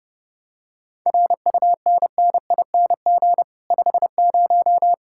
Morse code, heard as a keyed tone, about 28 wpm, 700 Hz. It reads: RUNNING 50